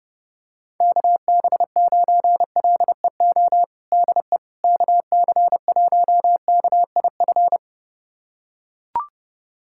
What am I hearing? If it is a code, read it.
Morse code, heard as a keyed tone, about 30 wpm, 700 Hz. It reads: KB9LEO DE KC1KIF
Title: USA Callsign DE Callsign - 6 Characters - 30wpm